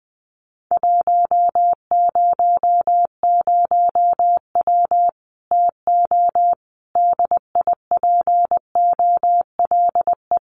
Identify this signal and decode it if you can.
Morse code, heard as a keyed tone, about 20 wpm, 700 Hz. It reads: 100W TO DIPOLE
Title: Common QSO Phrases 20wpm